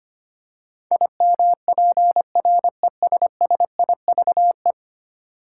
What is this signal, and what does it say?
Morse code, heard as a keyed tone, about 25 wpm, 700 Hz. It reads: IMPRESSIVE
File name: 10 Letter Words - First 3 Letters Given - 25wpm